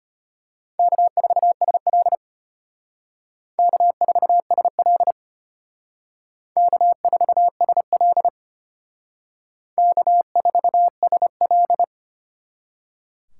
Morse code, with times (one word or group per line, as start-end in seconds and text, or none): 0.8-2.2 K4SL
3.6-5.1 K4SL
6.6-8.3 K4SL
9.8-11.8 K4SL